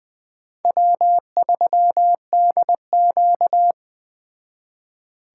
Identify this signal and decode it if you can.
Morse code, heard as a keyed tone, about 21 wpm, 700 Hz. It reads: W3DQ
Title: Speed Racing - USA 1x2 Call Signs 17wpm